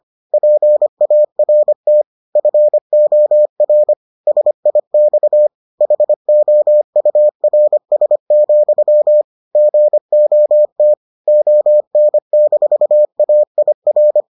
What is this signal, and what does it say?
Morse code, heard as a keyed tone, about 25 wpm, 600 Hz. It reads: PART FOR SIX HOURS, GOT ON-AIR